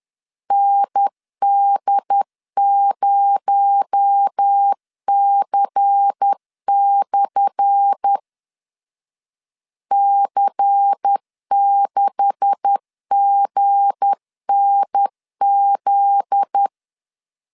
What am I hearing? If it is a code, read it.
Morse code, heard as a keyed tone, about 10 wpm, 800 Hz. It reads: ND0C/ C6GNZ
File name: LEZIONI / 10 SIGNS / 10A SLASH 10wpm 2808